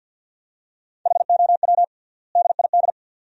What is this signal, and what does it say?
Morse code, heard as a keyed tone, about 50 wpm, 700 Hz. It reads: HOW DID
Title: 3x Word-Spacing - Sentences from Top 100 Words - 50wpm